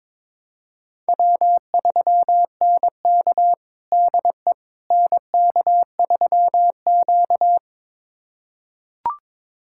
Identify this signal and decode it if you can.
Morse code, heard as a keyed tone, about 22 wpm, 700 Hz. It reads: W3NK DE NK3Q
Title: USA Callsign DE Callsign - 4 Characters - 22wpm